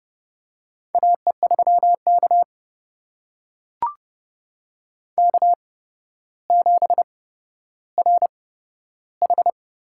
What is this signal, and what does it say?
Morse code, heard as a keyed tone, about 31 wpm, spaced as for 13 wpm, 700 Hz. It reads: AE3K K7RH
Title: USA Call Signs - 4 Characters - 8x Character Spacing - 30wpm